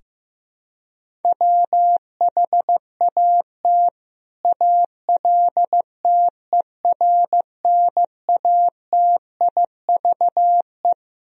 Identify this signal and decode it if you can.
Morse code, heard as a keyed tone, about 15 wpm, 700 Hz. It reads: WHAT ALTERNATIVE
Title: Sentences from Top 1000 Words - Review Part 4 - 15wpm